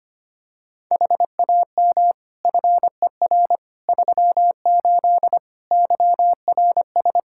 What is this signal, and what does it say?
Morse code, heard as a keyed tone, about 25 wpm, 700 Hz. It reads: HAM FER 38 YRS
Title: K1USN SST QSO Practice 25wpm